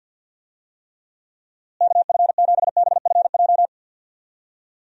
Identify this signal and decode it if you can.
Morse code, heard as a keyed tone, about 47 wpm, 700 Hz. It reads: KF7BFJ
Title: USA Call Signs - 6 Characters - Repeated 2x - 50wpm